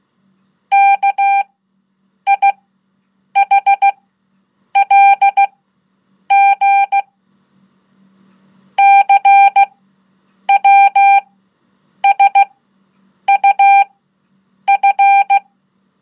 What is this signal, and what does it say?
Morse code, heard as a keyed tone, about 16 wpm, spaced as for 8 wpm, 790 Hz. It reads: KIHLG CWSUF
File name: Code / 5min-1